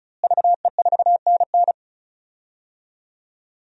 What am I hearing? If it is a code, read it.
Morse code, heard as a keyed tone, about 34 wpm, 700 Hz. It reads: VE4NN
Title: DXCC-100 Call Signs - 5 Characters - Repeated 2x - 35wpm